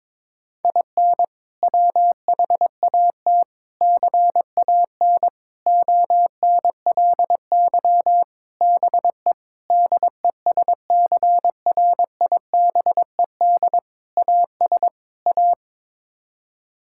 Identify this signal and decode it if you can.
Morse code, heard as a keyed tone, about 22 wpm, 700 Hz. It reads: IN WHAT CAN ONLY BE DESCRIBED AS A
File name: practice / 22wpm / Sets of 8 Words - 22wpm